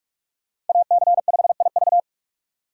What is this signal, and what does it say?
Morse code, heard as a keyed tone, about 46 wpm, 700 Hz. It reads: AC5IV
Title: USA Call Signs - 5 Characters - 4x Character Spacing - 45wpm